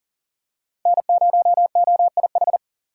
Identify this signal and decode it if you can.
Morse code, heard as a keyed tone, about 40 wpm, 700 Hz. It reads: N0OIH
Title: USA Call Signs - 5 Characters - 3x Character Spacing - 40wpm